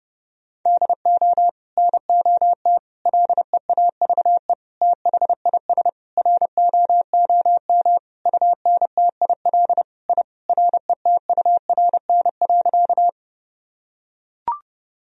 Morse code, heard as a keyed tone, about 30 wpm, 700 Hz. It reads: DO NOT LEAVE THIS ROOM UNTIL I RETURN.
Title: Sentences from Top 1000 Words - Review Part 1 - 30wpm